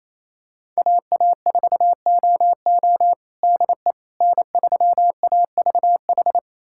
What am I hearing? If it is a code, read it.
Morse code, heard as a keyed tone, about 28 wpm, 700 Hz. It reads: AA4OO DE N3AVH
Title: USA Callsign DE Callsign - 5 Characters - 28wpm